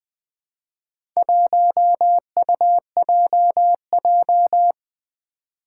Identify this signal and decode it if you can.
Morse code, heard as a keyed tone, about 20 wpm, 700 Hz. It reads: 1UJJ
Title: Four Challenging Letter-Number Combinations 20wpm